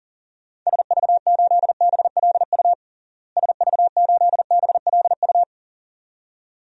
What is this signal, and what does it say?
Morse code, heard as a keyed tone, about 40 wpm, 700 Hz. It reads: SV8BLU SV8BLU
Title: DXCC-100 Call Signs - 6 Characters - Repeated 2x - 40wpm